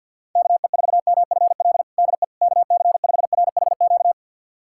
Morse code, heard as a keyed tone, about 52 wpm, 700 Hz. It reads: KE4MFL DE KC5RHQ
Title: USA Callsign DE Callsign 50wpm